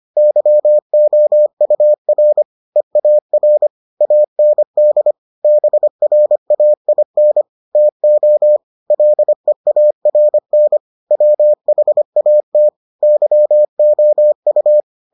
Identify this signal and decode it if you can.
Morse code, heard as a keyed tone, about 25 wpm, 600 Hz. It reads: YOUR EAR AND BRAIN TO LEARN WHAT YOU